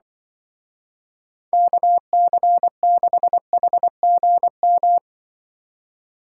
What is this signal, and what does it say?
Morse code, heard as a keyed tone, about 24 wpm, 700 Hz. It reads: KC6HGM
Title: Speed Racing - USA 2x3 Call Signs 20wpm